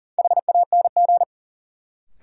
Morse code, heard as a keyed tone, about 40 wpm, 700 Hz. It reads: HANG
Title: Sentences from Top 1600 Words - New Word Drill Alt - 40wpm